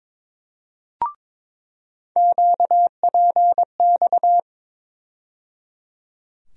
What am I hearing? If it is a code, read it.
Morse code, heard as a keyed tone, about 22 wpm, 700 Hz. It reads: QPX
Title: USA Call Sign Postfixes - 3 Letters 22wpm